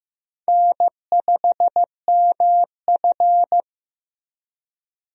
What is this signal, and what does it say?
Morse code, heard as a keyed tone, about 15 wpm, 700 Hz. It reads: N5MF
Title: USA Call Signs - 4 Characters - Repeated 2x - 15wpm